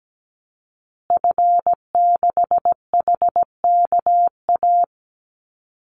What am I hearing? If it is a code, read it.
Morse code, heard as a keyed tone, about 17 wpm, 700 Hz. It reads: F6HKA
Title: Call Signs Repeated 2x - 17wpm